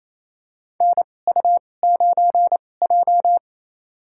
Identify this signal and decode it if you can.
Morse code, heard as a keyed tone, about 27 wpm, spaced as for 20 wpm, 700 Hz. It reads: NU9J
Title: USA Call Signs - 4 Characters - 2x Character Spacing - 28wpm